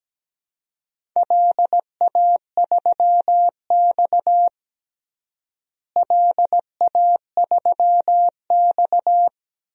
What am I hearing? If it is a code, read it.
Morse code, heard as a keyed tone, about 17 wpm, 700 Hz. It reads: LA3X LA3X
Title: DXCC-100 Call Signs - 4 Characters - Repeated 2x - 17wpm